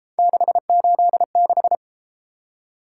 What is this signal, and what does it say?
Morse code, heard as a keyed tone, about 32 wpm, 700 Hz. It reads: 686